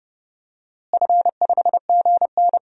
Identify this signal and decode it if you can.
Morse code, heard as a keyed tone, about 30 wpm, 700 Hz. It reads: F5GN
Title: DXCC-100 Call Signs - 4 Characters - Repeated 3x - 30wpm